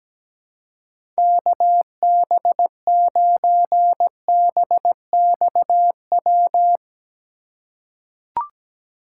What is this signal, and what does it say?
Morse code, heard as a keyed tone, about 17 wpm, 700 Hz. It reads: KB9BXW
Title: USA Call Signs - 6 Characters - 2x Character Spacing - 17wpm